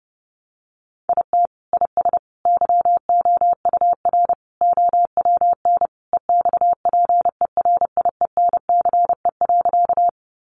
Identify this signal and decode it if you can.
Morse code, heard as a keyed tone, about 28 wpm, 700 Hz. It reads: IT IS YOUR OWN EXPERIENCE.
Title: Sentences from Top 600 Words 30wpm